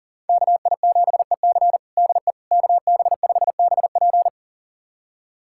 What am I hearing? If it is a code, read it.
Morse code, heard as a keyed tone, about 40 wpm, 700 Hz. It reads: KI7EC DE KB5BP
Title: USA Callsign DE Callsign - 5 Characters - 40wpm